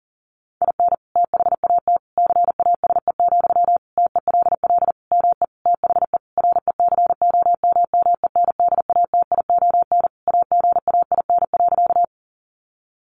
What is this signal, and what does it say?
Morse code, heard as a keyed tone, about 40 wpm, 700 Hz. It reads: IN THAT CASE, TELL ME THE RECOMMENDATION AGAIN.
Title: Sentences from Top 1400 Words 40wpm